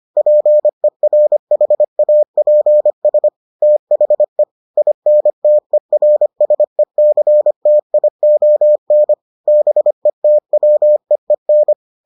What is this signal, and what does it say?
Morse code, heard as a keyed tone, about 25 wpm, 600 Hz. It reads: PERHAPS THE INTERSECTION BETWEEN